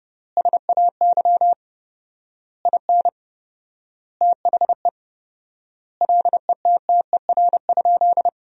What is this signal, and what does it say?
Morse code, heard as a keyed tone, about 32 wpm, 700 Hz. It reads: SAY IN THE LETTER?
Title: 4x Word-Spacing - Sentences from Top 300 Words - 30wpm